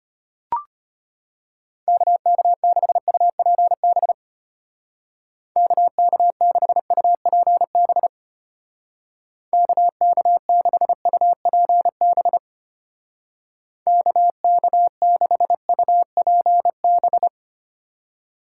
Morse code, heard as a keyed tone, about 33 wpm, 700 Hz. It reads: KK6UPB KK6UPB KK6UPB KK6UPB